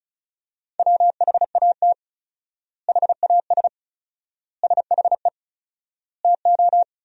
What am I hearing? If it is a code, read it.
Morse code, heard as a keyed tone, about 36 wpm, 700 Hz. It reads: WHAT HAS SHE TO
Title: 4x Word-Spacing - Sentences from Top 100 Words - 35wpm